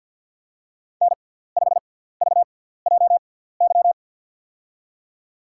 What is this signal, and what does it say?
Morse code, heard as a keyed tone, about 52 wpm, spaced as for 17 wpm, 700 Hz. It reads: N5VJY
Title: USA Call Signs - 6x Character Spacing - 50wpm